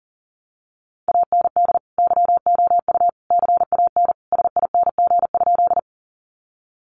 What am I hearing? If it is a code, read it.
Morse code, heard as a keyed tone, about 40 wpm, 700 Hz. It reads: AND YOU CAN SING?
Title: Sentences from Top 400 Words 40wpm